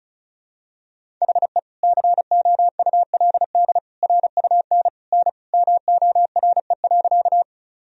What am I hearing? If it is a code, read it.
Morse code, heard as a keyed tone, about 35 wpm, 700 Hz. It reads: HE COULD RUN N MORE.
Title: Sentences from Top 300 Words and Abbreviations 35wpm